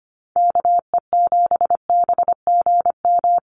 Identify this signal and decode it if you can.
Morse code, heard as a keyed tone, about 25 wpm, 700 Hz. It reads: KE7BGM
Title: NA SOTA Activator Callsigns 25wpm